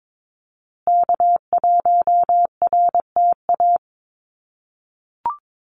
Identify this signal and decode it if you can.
Morse code, heard as a keyed tone, about 22 wpm, 700 Hz. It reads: K1RTA
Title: USA Call Signs Repeated 2x 22wpm